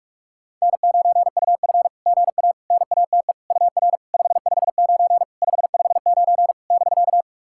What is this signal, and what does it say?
Morse code, heard as a keyed tone, about 44 wpm, 690 Hz. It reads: N9UF GA NATE UR 559 559 <BK>
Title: SOTA QSO Practice 45wpm